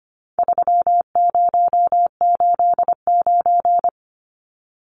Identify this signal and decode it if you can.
Morse code, heard as a keyed tone, about 25 wpm, 700 Hz. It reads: 3089
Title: Numbers - 4-Digit 25wpm